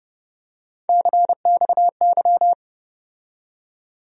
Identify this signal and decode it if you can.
Morse code, heard as a keyed tone, about 30 wpm, 700 Hz. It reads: CXY
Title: USA Call Sign Postfixes - 3 Letters 30wpm